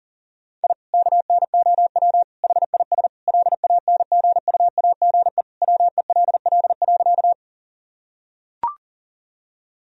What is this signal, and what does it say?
Morse code, heard as a keyed tone, about 40 wpm, 700 Hz. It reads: I KNOW HIS LANGUAGE WELL.
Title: Sentences from Top 1000 Words - Review Part 3 - 40wpm